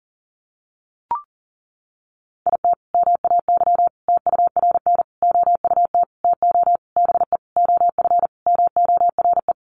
Morse code, heard as a keyed tone, about 40 wpm, 700 Hz. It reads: IT MAY TURN OUT TO BE OF MORE
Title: Sentences from Top 400 Words 40wpm